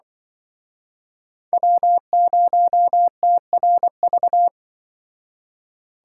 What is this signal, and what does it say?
Morse code, heard as a keyed tone, about 25 wpm, 700 Hz. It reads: W0TRV
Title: Speed Racing - USA 1x3 Call Signs 20wpm